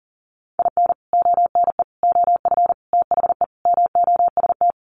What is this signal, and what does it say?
Morse code, heard as a keyed tone, about 40 wpm, 700 Hz. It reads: IN ONE OF THE MOST
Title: Sets of 5 Words Encore 40wpm